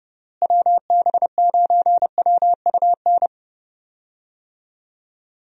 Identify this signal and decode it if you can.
Morse code, heard as a keyed tone, about 30 wpm, 700 Hz. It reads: WB9WUN
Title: USA Call Signs - 6 Characters - Repeated 2x - 30wpm